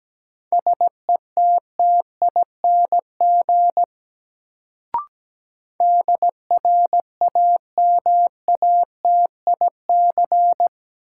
Morse code, heard as a keyed tone, about 17 wpm, 700 Hz. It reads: SETTING DRAMATIC
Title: Sentences from Top 1300 Words - New Word Drill Alt - 17wpm